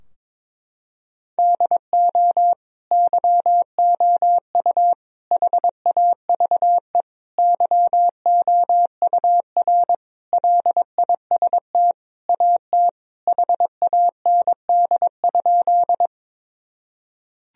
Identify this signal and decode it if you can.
Morse code, heard as a keyed tone, about 22 wpm, 700 Hz. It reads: DO YOU HAVE YOUR LIST AT HAND?
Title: Sentences from Top 1000 Words - Review Part 2 - 22wpm